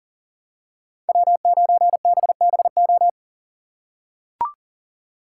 Morse code, heard as a keyed tone, about 40 wpm, 700 Hz. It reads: W9BBO
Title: USA Call Signs - 5 Characters - 2x Character Spacing - 40wpm